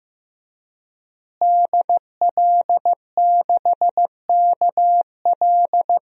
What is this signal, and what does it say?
Morse code, heard as a keyed tone, about 15 wpm, 700 Hz. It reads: DL6KL
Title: DXCC-100 Call Signs - 5 Characters - Repeated 3x - 15wpm